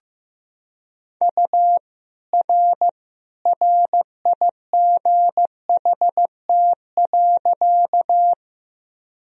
Morse code, heard as a keyed tone, about 15 wpm, 700 Hz. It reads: U R RIGHT.
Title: Sentences from Top 200 Words and Abbreviations 15wpm